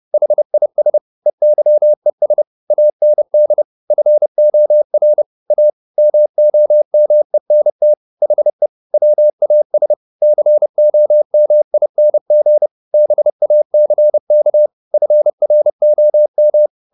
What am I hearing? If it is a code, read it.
Morse code, heard as a keyed tone, about 30 wpm, 600 Hz. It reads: HIS EYES AND FOR A MOMENT HE WAS COMING BACK FROM
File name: the old man and the sea / the old man and the sea 1 30wpm